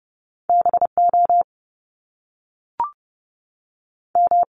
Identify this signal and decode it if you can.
Morse code, heard as a keyed tone, about 30 wpm, 700 Hz. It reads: B O M
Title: Top Two Letter Combinations in Words 30wpm